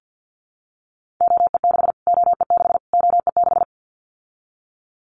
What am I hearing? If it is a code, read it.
Morse code, heard as a keyed tone, about 50 wpm, 700 Hz. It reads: OE6 OE6 OE6
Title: Callsign Prefixes for Top 25 DXCC - Repeated 3x - 50wpm